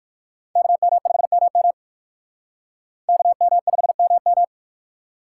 Morse code, heard as a keyed tone, about 56 wpm, 700 Hz. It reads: KM5MM KM5MM
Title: Speed Racing - USA Call Signs 45wpm